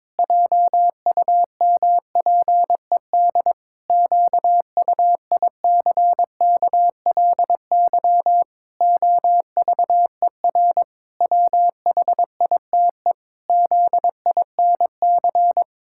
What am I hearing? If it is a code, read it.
Morse code, heard as a keyed tone, about 22 wpm, 700 Hz. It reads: JUMPED QUICKLY OVER WHITE ZINC
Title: Pangram Challenge v2 - 22wpm